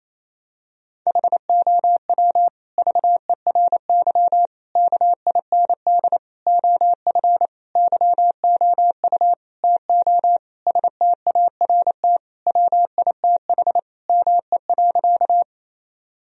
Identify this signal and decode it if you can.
Morse code, heard as a keyed tone, about 28 wpm, 700 Hz. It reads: HOW VERY KIND OF YOU TO START WITH ME.
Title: Sentences from Top 1000 Words - Review Part 1 - 28wpm